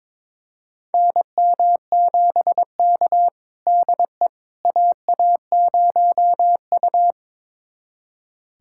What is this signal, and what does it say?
Morse code, heard as a keyed tone, about 22 wpm, 700 Hz. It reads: NM7K DE AA0U
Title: USA Callsign DE Callsign - 4 Characters - 22wpm